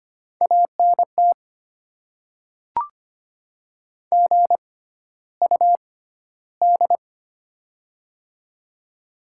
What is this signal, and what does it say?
Morse code, heard as a keyed tone, about 25 wpm, spaced as for 13 wpm, 700 Hz. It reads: ANT GUD
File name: Top 20 QSO Elements - 6x Character Spacing - 25wpm